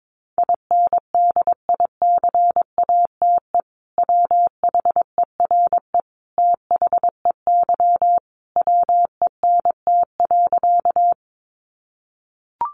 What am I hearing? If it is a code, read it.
Morse code, heard as a keyed tone, about 22 wpm, 700 Hz. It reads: INDICATE WHERE THEY WENT.
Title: Sentences from Top 1300 Words 22wpm